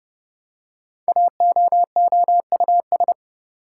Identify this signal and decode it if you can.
Morse code, heard as a keyed tone, about 30 wpm, 700 Hz. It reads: AOOUS
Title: Code Groups - 5 Letters - 30wpm